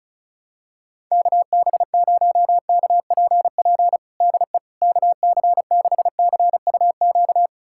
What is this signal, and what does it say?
Morse code, heard as a keyed tone, about 34 wpm, 700 Hz. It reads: KB0KPP DE KC6CUQ
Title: USA Callsign DE Callsign 35wpm